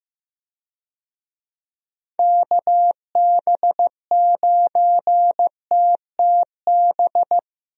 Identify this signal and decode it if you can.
Morse code, heard as a keyed tone, about 15 wpm, 700 Hz. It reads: KB9TTB
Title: USA Call Signs - 6 Characters - Repeated 2x - 15wpm